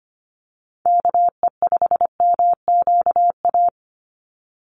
Morse code, heard as a keyed tone, about 25 wpm, 700 Hz. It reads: KE5MQA
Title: USA Call Signs 25wpm